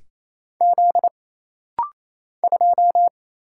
Morse code, heard as a keyed tone, about 27 wpm, 700 Hz. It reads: Z 2
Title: Single Letter-Number Alt - Rapid-Fire 28wpm